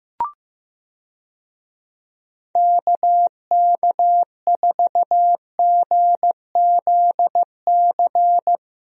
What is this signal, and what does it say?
Morse code, heard as a keyed tone, about 15 wpm, 700 Hz. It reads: KK4GZC
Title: USA Call Signs - 6 Characters - Repeated 2x - 15wpm